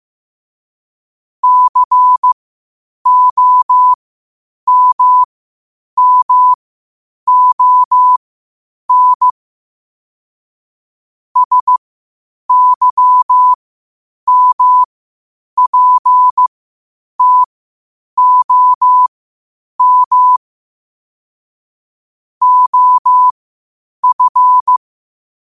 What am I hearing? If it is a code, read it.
Morse code, heard as a keyed tone, about 15 wpm, spaced as for 8 wpm, 1000 Hz. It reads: COMMON SYMPTOM OF
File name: qotdfiles / 8wpm